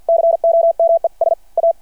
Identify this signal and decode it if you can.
Morse code, heard as a keyed tone, about 50 wpm, 670 Hz. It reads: COMES A
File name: archived / CRSnet